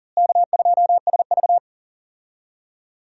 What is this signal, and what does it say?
Morse code, heard as a keyed tone, about 40 wpm, 700 Hz. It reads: K2SV
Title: USA Call Signs - 4 Characters - Repeated 3x - 40wpm